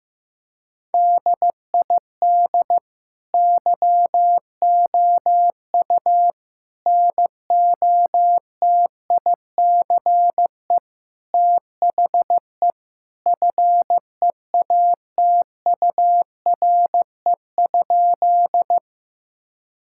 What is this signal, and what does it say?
Morse code, heard as a keyed tone, about 15 wpm, 700 Hz. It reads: DID YOU NOTICE THE FEATURE?
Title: Sentences from Top 1000 Words - Review Part 5 - 15wpm